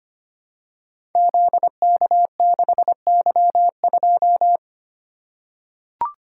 Morse code, heard as a keyed tone, about 25 wpm, 700 Hz. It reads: ZK6Y2
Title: Code Groups - 5 Letter:Numbers - 25wpm